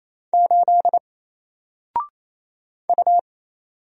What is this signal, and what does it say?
Morse code, heard as a keyed tone, about 27 wpm, 700 Hz. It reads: 8 U
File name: Single-Letter-Number-28wpm